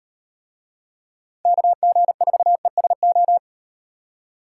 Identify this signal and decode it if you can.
Morse code, heard as a keyed tone, about 37 wpm, 700 Hz. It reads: KG4ESO